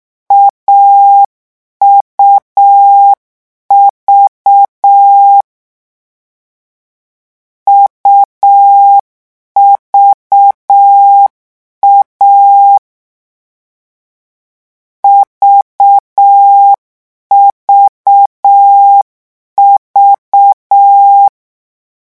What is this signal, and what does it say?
Morse code, heard as a keyed tone, about 6 wpm, 800 Hz. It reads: AUV UVA VVV
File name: morse mp3 / lecon3-vitesse 6